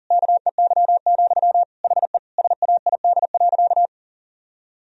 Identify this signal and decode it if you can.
Morse code, heard as a keyed tone, about 40 wpm, 700 Hz. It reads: KEY, HE SAID.
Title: Sentences from Top 1000 Words - Review Part 4 - 40wpm